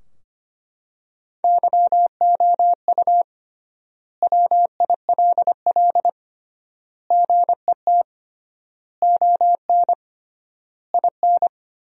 Morse code, heard as a keyed tone, about 25 wpm, 700 Hz. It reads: YOU WILL GET ON IN
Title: 3x Word-Spacing - Sentences from Top 200 Words - 25wpm